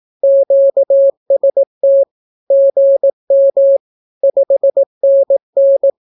Morse code, heard as a keyed tone, about 18 wpm, 550 Hz. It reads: QST GM 5NN
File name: naukapodstaw / skroty / skroty